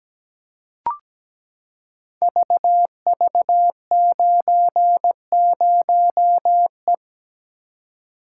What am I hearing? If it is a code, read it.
Morse code, heard as a keyed tone, about 17 wpm, 700 Hz. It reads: VV90E
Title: Code Groups - 5 Letter:Numbers - 17wpm